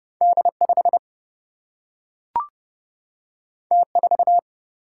Morse code, heard as a keyed tone, about 30 wpm, 700 Hz. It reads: D5 T4
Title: Single-Letter Single-Number 30wpm